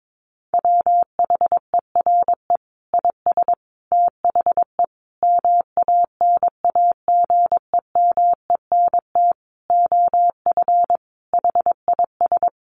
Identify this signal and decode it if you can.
Morse code, heard as a keyed tone, about 22 wpm, 700 Hz. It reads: WHERE IS THE MANAGEMENT OF HIS
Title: Sentences from Top 1200 Words 22wpm